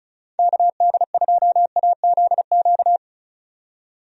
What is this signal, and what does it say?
Morse code, heard as a keyed tone, about 36 wpm, 700 Hz. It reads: KD2AZQ
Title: USA Call Signs - 6 Characters - 35wpm